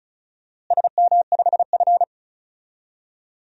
Morse code, heard as a keyed tone, about 36 wpm, 700 Hz. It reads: SM5F
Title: DXCC-100 Call Signs - 4 Characters - Repeated 2x - 35wpm